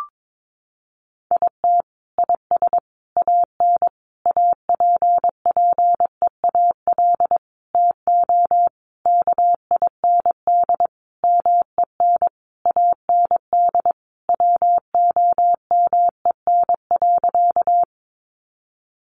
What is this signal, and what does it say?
Morse code, heard as a keyed tone, about 22 wpm, 700 Hz. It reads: IT IS AN APPEAL TO KIND MEN AND WOMEN.
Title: Sentences from Top 1300 Words 22wpm